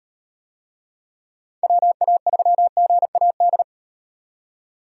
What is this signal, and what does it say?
Morse code, heard as a keyed tone, about 38 wpm, 700 Hz. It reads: WA3GAD